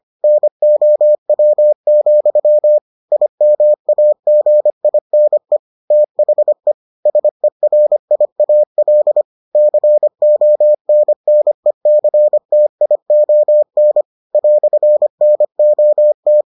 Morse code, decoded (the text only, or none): NOW, IMAGINE THE SERIAL CONNECTION "NOT